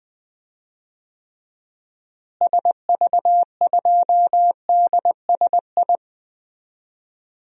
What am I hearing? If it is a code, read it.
Morse code, heard as a keyed tone, about 20 wpm, 700 Hz. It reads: SV2DSI
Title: DXCC-100 Call Signs - 6 Characters - Repeated 3x - 20wpm